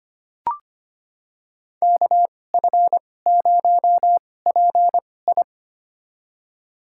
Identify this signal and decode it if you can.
Morse code, heard as a keyed tone, about 25 wpm, spaced as for 18 wpm, 700 Hz. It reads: KF0PI
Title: USA Call Signs - 5 Characters - 2x Character Spacing - 25wpm